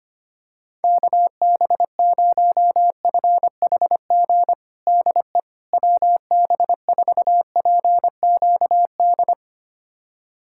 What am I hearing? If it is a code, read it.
Morse code, heard as a keyed tone, about 25 wpm, 700 Hz. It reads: KB0FHG DE WB4PQD
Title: USA Callsign DE Callsign - 6 Characters - 25wpm